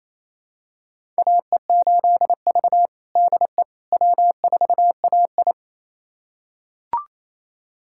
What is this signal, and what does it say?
Morse code, heard as a keyed tone, about 27 wpm, 690 Hz. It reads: AE8V DE W4AI
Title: USA Callsign DE Callsign - 4 Characters - 28wpm